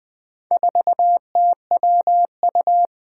Morse code, heard as a keyed tone, about 20 wpm, 700 Hz. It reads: 4TWU
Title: Four Challenging Letter-Number Combinations 20wpm